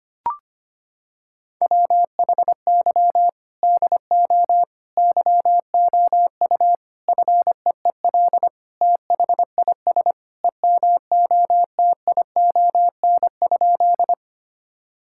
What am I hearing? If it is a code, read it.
Morse code, heard as a keyed tone, about 25 wpm, 700 Hz. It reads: WHY DO YOU FEEL THIS EMOTION?
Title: Sentences from Top 1600 Words 25wpm